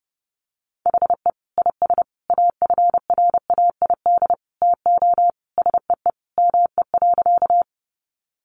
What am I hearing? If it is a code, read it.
Morse code, heard as a keyed tone, about 30 wpm, 700 Hz. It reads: HE IS AFRAID TO SEE ME.
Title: Sentences from Top 1500 Words 30wpm